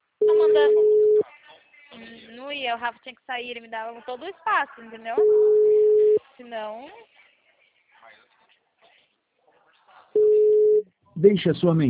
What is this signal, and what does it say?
Ringback tone (Europe, 425 Hz), with speech.
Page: Sorocaba 2